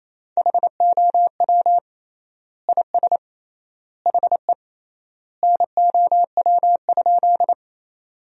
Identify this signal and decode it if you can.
Morse code, heard as a keyed tone, about 28 wpm, 700 Hz. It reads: HOW IS HE NOW?
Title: 3x Word-Spacing - Sentences from Top 100 Words - 28wpm